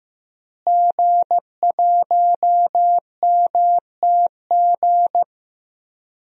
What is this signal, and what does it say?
Morse code, heard as a keyed tone, about 15 wpm, 700 Hz. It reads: G1MTG
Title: DXCC-100 Call Signs - 5 Characters - Repeated 2x - 15wpm